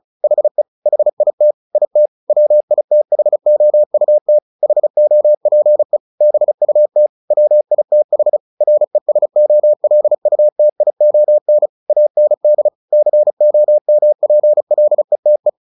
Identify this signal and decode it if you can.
Morse code, heard as a keyed tone, about 34 wpm, 600 Hz. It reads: HE HIT IT WITHOUT HOPE BUT WITH RESOLUTION AND COMPLETE
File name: the old man and the sea 2 35wpm